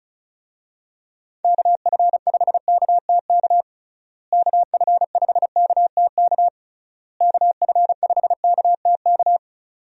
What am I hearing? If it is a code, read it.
Morse code, heard as a keyed tone, about 36 wpm, 700 Hz. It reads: KF5KTK KF5KTK KF5KTK
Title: USA Call Signs - 6 Characters - Repeated 3x - 35wpm